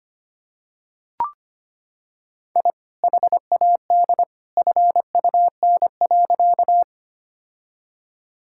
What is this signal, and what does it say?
Morse code, heard as a keyed tone, about 25 wpm, 700 Hz. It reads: I HAD FUN.
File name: practice / 25wpm / Sentences with 2-Word Phrases - 25wpm